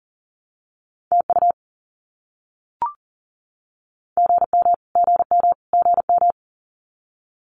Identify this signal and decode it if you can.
Morse code, heard as a keyed tone, about 40 wpm, 700 Hz. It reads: TU GM GM GM
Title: Abbreviations Repeated 3x - 40wpm